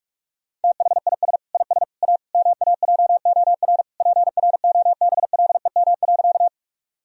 Morse code, heard as a keyed tone, about 46 wpm, 700 Hz. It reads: THIS IS A MAJOR PROBLEM.
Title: Sentences from Top 1000 Words - Review Part 4 - 45wpm